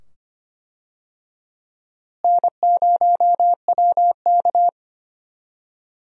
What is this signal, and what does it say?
Morse code, heard as a keyed tone, about 25 wpm, 700 Hz. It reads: N0WK
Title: USA Call Signs - 4 Characters - Repeated 2x - 25wpm